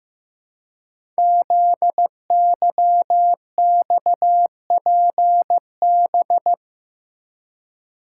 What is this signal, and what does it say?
Morse code, heard as a keyed tone, about 15 wpm, 700 Hz. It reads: ZYXPB
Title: Code Groups - 5 Letter:Numbers - 15wpm